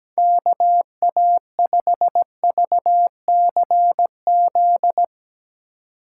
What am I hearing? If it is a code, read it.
Morse code, heard as a keyed tone, about 17 wpm, 700 Hz. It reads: KA5VCZ
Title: USA Call Signs - 6 Characters - 4x Character Spacing - 17wpm